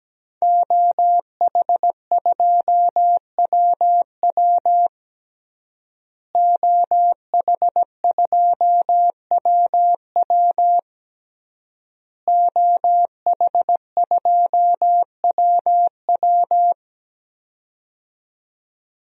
Morse code, heard as a keyed tone, about 17 wpm, 700 Hz. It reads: OH2WW OH2WW OH2WW
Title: DXCC-100 Call Signs - 5 Characters - Repeated 3x - 17wpm